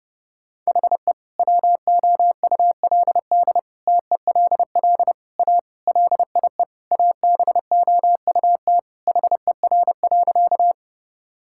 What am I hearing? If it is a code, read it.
Morse code, heard as a keyed tone, about 32 wpm, 700 Hz. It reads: HE WOULD TELL A LIE ABOUT HER.
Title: Sentences from Top 1600 Words 30wpm